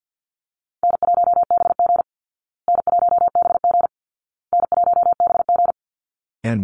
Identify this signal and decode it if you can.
Morse code, heard as a keyed tone, about 50 wpm, 700 Hz. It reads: N1BG N1BG N1BG
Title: Easy Call Signs Repeated 3x - 50wpm